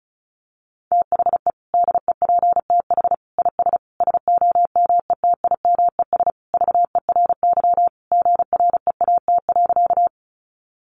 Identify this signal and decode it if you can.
Morse code, heard as a keyed tone, about 34 wpm, 700 Hz. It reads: THE DEPTH IS SOMETIMES VERY GREAT.
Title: Sentences from Top 1200 Words 35wpm